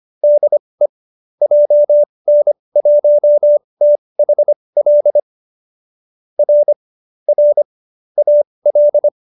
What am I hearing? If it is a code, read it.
Morse code, heard as a keyed tone, about 25 wpm, spaced as for 20 wpm, 590 Hz. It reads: DE JN1THL R R AL